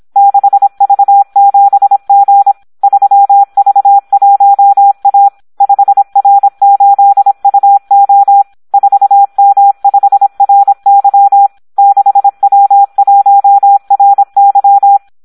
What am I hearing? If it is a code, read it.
Morse code, heard as a keyed tone, about 26 wpm, 800 Hz. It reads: 6V7G 3V1A 5R8UO 4M5RY 6W1RY
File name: Audio practica CW / SD 1 26